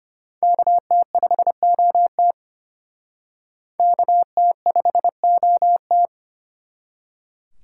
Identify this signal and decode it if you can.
Morse code, heard as a keyed tone, about 33 wpm, 700 Hz. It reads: KT5OT KT5OT